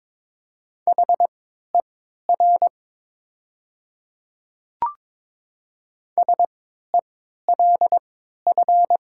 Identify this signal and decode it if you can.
Morse code, heard as a keyed tone, about 22 wpm, spaced as for 13 wpm, 700 Hz. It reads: HER SELF
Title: 3x Character Spacing - Top 200 Words - 22wpm